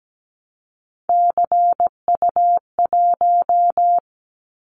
Morse code, heard as a keyed tone, about 17 wpm, 700 Hz. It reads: CU1
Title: Callsign Prefixes for Top 100 DXCC - 17wpm